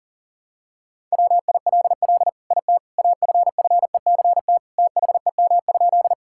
Morse code, heard as a keyed tone, about 37 wpm, 700 Hz. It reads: WILL IT AFFECT THEM?
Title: Sentences from Top 1000 Words - Review Part 6 - 40wpm